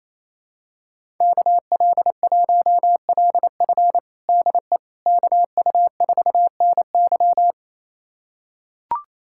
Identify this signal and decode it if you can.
Morse code, heard as a keyed tone, about 28 wpm, 700 Hz. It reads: KL1LF DE KU4NY
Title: USA Callsign DE Callsign - 5 Characters - 28wpm